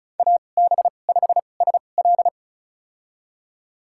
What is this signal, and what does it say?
Morse code, heard as a keyed tone, about 36 wpm, spaced as for 25 wpm, 700 Hz. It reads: AB5SL
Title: USA Call Signs - 5 Characters - 2x Character Spacing - 35wpm